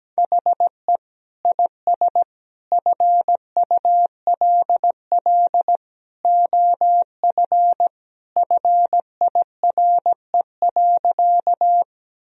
Morse code, heard as a keyed tone, about 17 wpm, 700 Hz. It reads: HE IS FULL OF FIRE.
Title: Sentences from Top 1000 Words - Review Part 3 - 17wpm